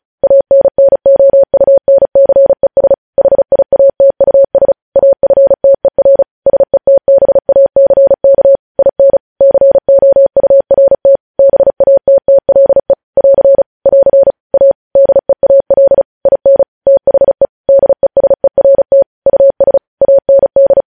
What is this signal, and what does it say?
Morse code, heard as a keyed tone, about 36 wpm, 550 Hz. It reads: ANNOUNCES HIATUS AFTER SETBACK IN COURT BATTLE <AR> <AR> A DEAL IN THE DESERT US AND
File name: podcast / News headlines-1x-35x35